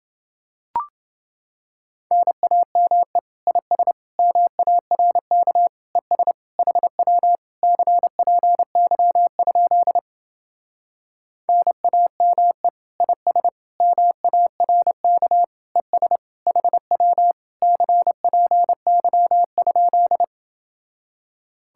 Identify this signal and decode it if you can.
Morse code, heard as a keyed tone, about 28 wpm, 700 Hz. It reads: NAME IS MARK ES HW CPY? NAME IS MARK ES HW CPY?